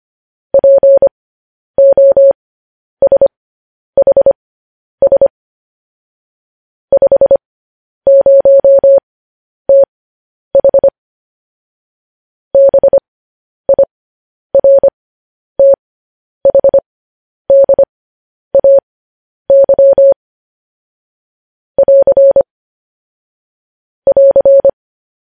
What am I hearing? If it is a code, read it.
Morse code, heard as a keyed tone, about 25 wpm, spaced as for 10 wpm, 550 Hz. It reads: POSHS 50TH BIRTHDAY <AR> <AR>
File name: podcast / News headlines-1x-25x10